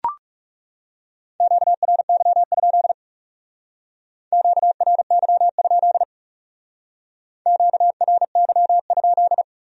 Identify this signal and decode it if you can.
Morse code, heard as a keyed tone, about 37 wpm, 700 Hz. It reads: QRY? QRY? QRY?
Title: Speed-Racing Q-Codes - 30wpm